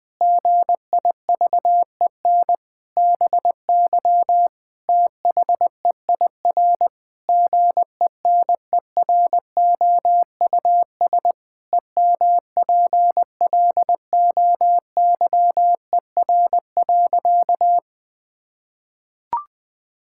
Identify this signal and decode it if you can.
Morse code, heard as a keyed tone, about 21 wpm, 700 Hz. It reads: GIVEN BY THEIR GENEROUS EMPLOYER.
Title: Pangram Challenge v2 - 20wpm